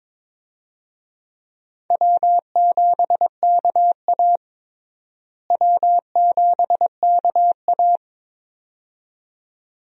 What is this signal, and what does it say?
Morse code, heard as a keyed tone, about 22 wpm, 700 Hz. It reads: W7KA W7KA
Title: USA Call Signs - 4 Characters - Repeated 2x - 22wpm